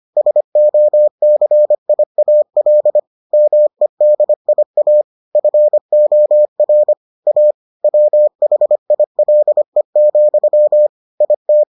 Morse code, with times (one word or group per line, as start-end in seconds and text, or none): 0.2-3.0 SOCIAL
3.3-5.0 MEDIA
5.4-6.9 FOR
7.3-7.5 A
7.8-10.9 WHILE,
11.2-11.6 IT